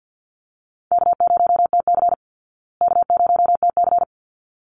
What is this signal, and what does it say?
Morse code, heard as a keyed tone, about 49 wpm, 700 Hz. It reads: K0TC K0TC
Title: Call Signs Repeated 3x - 50wpm